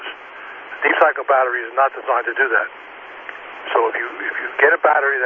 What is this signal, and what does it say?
Emergency Alert System (SAME) tones, heard as harsh buzzing bursts, with speech. It is heard from 0.3 to 4.3 s.